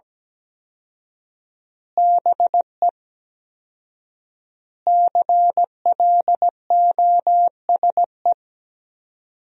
Morse code, heard as a keyed tone, about 17 wpm, 700 Hz. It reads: BE CLOSE
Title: 4x Word-Spacing - Sentences from Top 300 Words - 17wpm